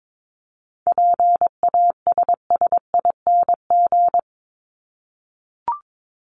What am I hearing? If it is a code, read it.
Morse code, heard as a keyed tone, about 22 wpm, 700 Hz. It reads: PASSING
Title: ING Suffix - Top 100 Words with Base Words 22wpm